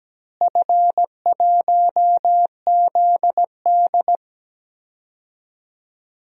Morse code, heard as a keyed tone, about 17 wpm, 700 Hz. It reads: F1ZD
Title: DXCC-100 Call Signs - 4 Characters - Repeated 2x - 17wpm